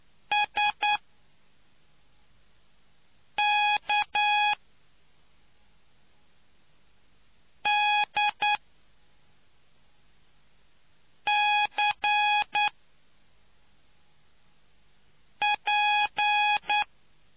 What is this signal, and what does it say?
Morse code, heard as a keyed tone, about 9 wpm, spaced as for 3 wpm, 830 Hz. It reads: SKDCP